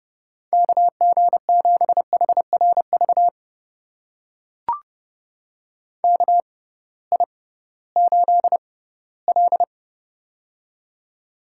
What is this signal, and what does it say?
Morse code, heard as a keyed tone, about 32 wpm, spaced as for 18 wpm, 700 Hz. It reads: KG7HRV KI8L
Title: USA Call Signs - 6x Character Spacing - 30wpm